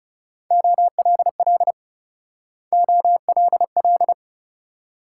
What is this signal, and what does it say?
Morse code, heard as a keyed tone, about 37 wpm, 700 Hz. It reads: OLL OLL
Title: Speed-Racing Multi-letter Phonemes - 30wpm